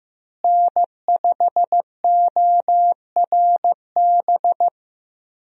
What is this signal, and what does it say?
Morse code, heard as a keyed tone, about 15 wpm, 700 Hz. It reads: N5ORB
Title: USA Call Signs - 5 Characters - Repeated 2x - 15wpm